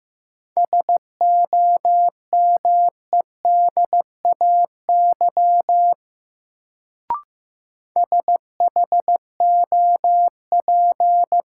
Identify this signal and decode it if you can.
Morse code, heard as a keyed tone, about 15 wpm, 700 Hz. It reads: SOMEDAY SHOP
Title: Sentences from Top 1600 Words - New Word Drill Alt - 15wpm